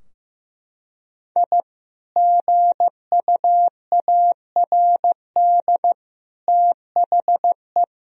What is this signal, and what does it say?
Morse code, heard as a keyed tone, about 15 wpm, 700 Hz. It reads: I GUARD THE
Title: Sentences from Top 1000 Words - Review Part 4 - 15wpm